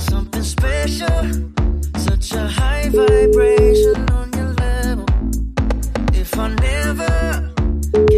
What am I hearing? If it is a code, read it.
Ringback tone (Europe, 425 Hz), with noise.